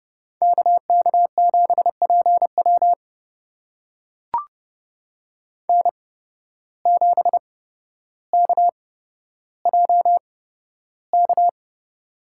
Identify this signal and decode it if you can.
Morse code, heard as a keyed tone, about 29 wpm, spaced as for 13 wpm, 700 Hz. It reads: KK7PW N7KJK
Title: USA Call Signs - 5 Characters - 8x Character Spacing - 30wpm